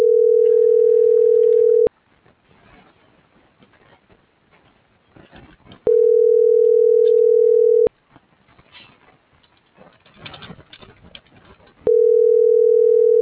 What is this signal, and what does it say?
Ringback tone (North America), 440 + 480 Hz.